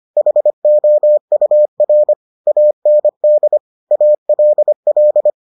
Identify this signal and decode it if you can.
Morse code, heard as a keyed tone, about 25 wpm, 600 Hz. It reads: HOUR AND ALL